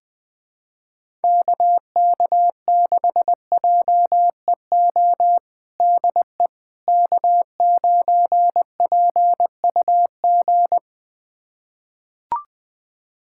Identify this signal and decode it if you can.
Morse code, heard as a keyed tone, about 20 wpm, 700 Hz. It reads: KK6JEO DE K9PUG
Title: USA Callsign DE Callsign 20wpm